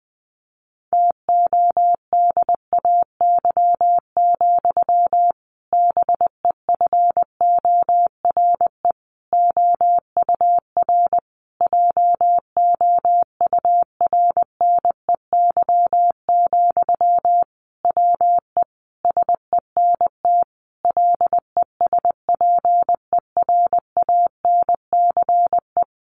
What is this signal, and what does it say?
Morse code, heard as a keyed tone, about 20 wpm, 700 Hz. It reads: TODAY, BEFORE OUR JOURNEY, WE SENT LESPERANCE
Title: A Sound Of Thunder 20wpm